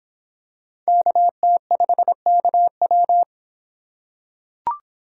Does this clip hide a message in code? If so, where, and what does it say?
Yes: Morse code from 0.9 to 3.2 s, heard as a keyed tone, about 24 wpm, 700 Hz. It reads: KT5KW